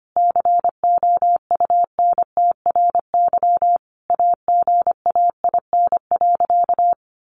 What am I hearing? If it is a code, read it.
Morse code, heard as a keyed tone, about 25 wpm, 700 Hz. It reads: COUNTRY AGAIN.
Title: Sentences from Top 1200 Words 25wpm